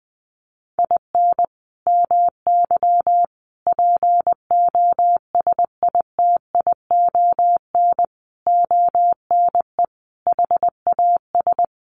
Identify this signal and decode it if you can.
Morse code, heard as a keyed tone, about 20 wpm, 700 Hz. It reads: IN MY POSITION ONE HAS
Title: Sentences from Top 1100 Words 20wpm